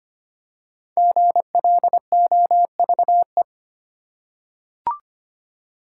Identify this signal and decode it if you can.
Morse code, heard as a keyed tone, about 25 wpm, 690 Hz. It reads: GLOVE
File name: practice / 25wpm / Easy Words with Hints - 25wpm